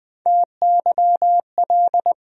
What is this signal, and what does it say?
Morse code, heard as a keyed tone, about 20 wpm, 700 Hz. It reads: TYL
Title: USA Call Sign Postfixes - 3 Letters 20wpm